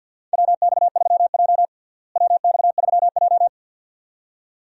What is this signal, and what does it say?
Morse code, heard as a keyed tone, about 52 wpm, 700 Hz. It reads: WX3J WX3J
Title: USA Call Signs - 4 Characters - Repeated 2x - 50wpm